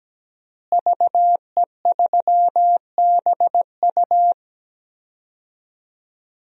Morse code, heard as a keyed tone, about 17 wpm, 700 Hz. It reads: VE3BU
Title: DXCC-100 Call Signs - 5 Characters - Repeated 3x - 17wpm